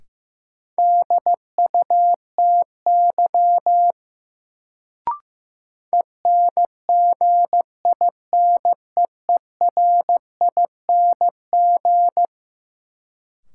Morse code, heard as a keyed tone, about 16 wpm, 700 Hz. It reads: DUTY ENGINEERING
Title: Sentences from Top 1500 Words - New Word Drill Alt - 15wpm